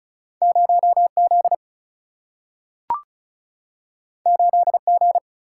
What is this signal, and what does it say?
Morse code, heard as a keyed tone, about 36 wpm, 700 Hz. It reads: 0Z 8G
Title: Single-Letter Single-Number 35wpm